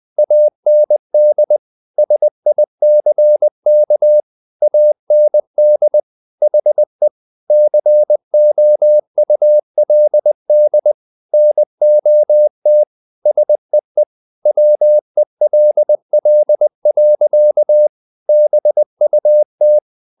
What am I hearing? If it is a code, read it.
Morse code, heard as a keyed tone, about 20 wpm, 600 Hz. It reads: AND SICK AND HE COULD NOT SEE WELL. BUT
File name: the old man and the sea part 2 20wpm